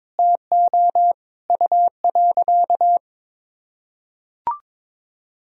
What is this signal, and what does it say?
Morse code, heard as a keyed tone, about 21 wpm, 700 Hz. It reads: TO U.
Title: Sentences from Top 500 Words and Abbreviations 22wpm